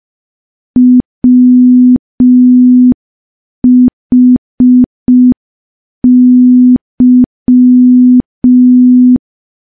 Morse code, heard as a keyed tone, about 5 wpm, 250 Hz. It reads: WHY
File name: morse / morse-training-05WPM